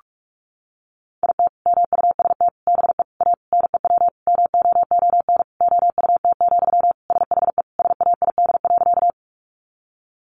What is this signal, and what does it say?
Morse code, heard as a keyed tone, about 44 wpm, 700 Hz. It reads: IT MUST BE A NEW MOON OUT, SHE SAID.
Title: Sentences from Top 500 Words 45wpm